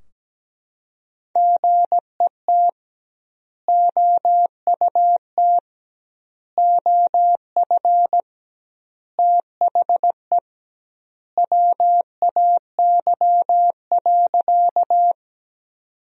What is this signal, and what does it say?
Morse code, heard as a keyed tone, about 17 wpm, 700 Hz. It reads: GET OUT OF THE WAY.
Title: 2x Word-Spacing - Sentences from Top 200 Words - 17wpm